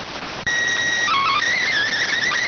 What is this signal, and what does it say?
SSTV (slow-scan television), PD 290.